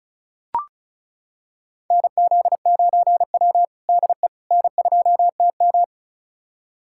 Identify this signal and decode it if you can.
Morse code, heard as a keyed tone, about 36 wpm, 700 Hz. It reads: NZ9W DE N2TM
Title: USA Callsign DE Callsign - 4 Characters - 35wpm